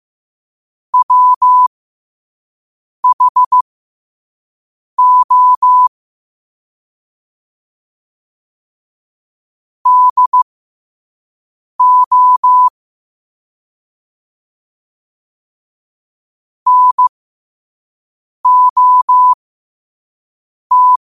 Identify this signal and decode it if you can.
Morse code, heard as a keyed tone, about 15 wpm, spaced as for 5 wpm, 1000 Hz. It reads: WHO DO NOT
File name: qotdfiles / 5wpm